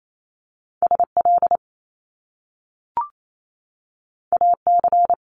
Morse code, heard as a keyed tone, about 28 wpm, 700 Hz. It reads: SL AC
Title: Top Two Letter Combinations in Words 28wpm